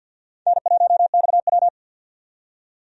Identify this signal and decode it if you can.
Morse code, heard as a keyed tone, about 49 wpm, 700 Hz. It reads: N1XW
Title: USA Call Signs - 4 Characters - 50wpm